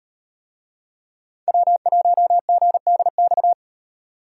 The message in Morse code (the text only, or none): W1GDX